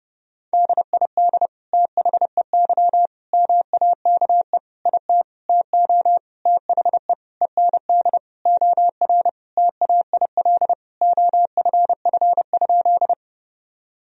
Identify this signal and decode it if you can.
Morse code, heard as a keyed tone, about 28 wpm, 700 Hz. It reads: DID THEY MAKE IT TO THE END OR TAIL OFF?
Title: Sentences from Top 1000 Words - Review Part 3 - 30wpm